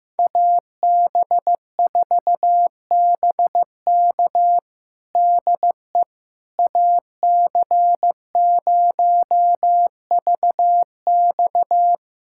Morse code, heard as a keyed tone, about 15 wpm, 700 Hz. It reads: AB4BK DE AC0VX
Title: USA Callsign DE Callsign - 5 Characters - 15wpm